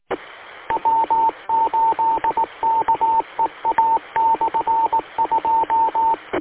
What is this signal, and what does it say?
Morse code, heard as a keyed tone, about 18 wpm, 900 Hz, with radio noise. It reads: W8KEA/2